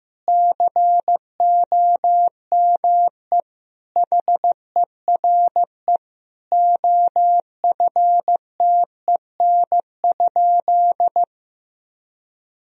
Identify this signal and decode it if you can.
Morse code, heard as a keyed tone, about 15 wpm, 700 Hz. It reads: COME HERE OFTEN?
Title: Sentences from Top 1000 Words - Review Part 4 - 15wpm